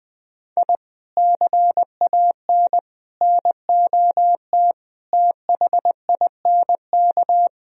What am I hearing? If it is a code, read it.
Morse code, heard as a keyed tone, about 20 wpm, 700 Hz. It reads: I CAN NOT THINK